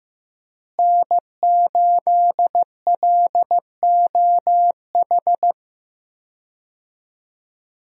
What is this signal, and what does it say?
Morse code, heard as a keyed tone, about 15 wpm, 700 Hz. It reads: N8LOH
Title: USA Call Signs - 5 Characters - Repeated 2x - 15wpm